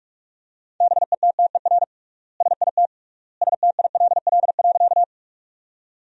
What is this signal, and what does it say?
Morse code, heard as a keyed tone, about 46 wpm, 700 Hz. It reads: BETTER SIT STILL.
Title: 3x Word-Spacing - Sentences from Top 500 Words - 45wpm